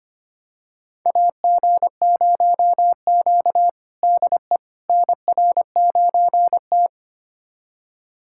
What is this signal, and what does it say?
Morse code, heard as a keyed tone, about 25 wpm, 700 Hz. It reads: AG0Q DE NR9T
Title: USA Callsign DE Callsign - 4 Characters - 25wpm